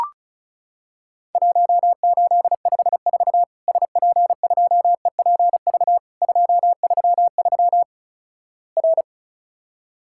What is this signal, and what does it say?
Morse code, heard as a keyed tone, about 36 wpm, 700 Hz. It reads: 1854 SP2EPV 233 R
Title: Worked All Europe DX Contest - QTC Exchange - 35wpm